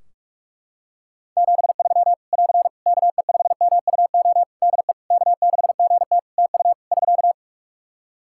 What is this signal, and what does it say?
Morse code, heard as a keyed tone, about 44 wpm, 700 Hz. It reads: 73 <AR> KE5MUO DE K6GT TU <SK>
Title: Practice QSO - Exchange 4 - Ending - 45wpm